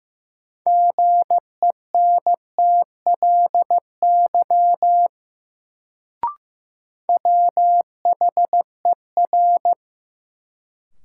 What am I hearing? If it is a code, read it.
Morse code, heard as a keyed tone, about 15 wpm, 700 Hz. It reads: GENTLY WHER
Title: Sentences from Top 1500 Words - New Word Drill Alt - 15wpm